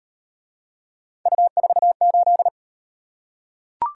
Morse code, heard as a keyed tone, about 38 wpm, 700 Hz. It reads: U48